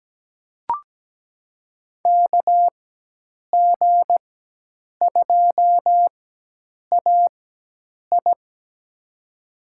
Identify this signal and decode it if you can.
Morse code, heard as a keyed tone, about 17 wpm, spaced as for 8 wpm, 700 Hz. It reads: KG2AI
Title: USA Call Signs - 5 Characters - 4x Character Spacing - 17wpm